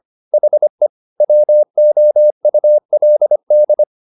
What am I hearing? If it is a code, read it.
Morse code, heard as a keyed tone, about 25 wpm, 600 Hz. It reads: HE WOULD